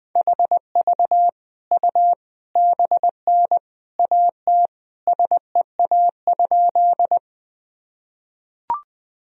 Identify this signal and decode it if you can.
Morse code, heard as a keyed tone, about 21 wpm, 700 Hz. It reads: HV U BN AT SEA?
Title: Sentences from Top 300 Words and Abbreviations 20wpm